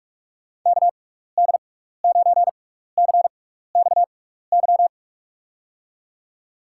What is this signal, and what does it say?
Morse code, heard as a keyed tone, about 46 wpm, 700 Hz. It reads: K D 9 C X Y
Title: USA Call Signs - 6 Characters - 6x Character Spacing - 45wpm